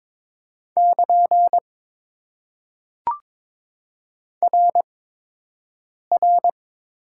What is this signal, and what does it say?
Morse code, heard as a keyed tone, about 22 wpm, spaced as for 6 wpm, 700 Hz. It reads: <KN> RR
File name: Top 90 QSO Elements - 8x Character Spacing - 22wpm